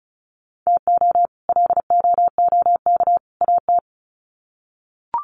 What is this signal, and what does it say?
Morse code, heard as a keyed tone, about 34 wpm, 700 Hz. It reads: TO LOOK AT
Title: Sets of 3 Words 35wpm